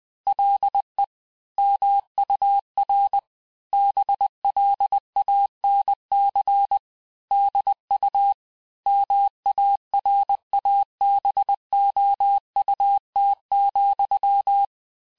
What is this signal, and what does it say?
Morse code, heard as a keyed tone, about 20 wpm, 800 Hz. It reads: LE MUR BLANC DU MARABOUT,
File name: cours-du-mercredi / audios / cours-870-texte-CLAIR-vitesse-19-MPM